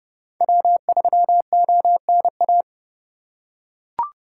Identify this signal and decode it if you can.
Morse code, heard as a keyed tone, about 29 wpm, 700 Hz. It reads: W3ONA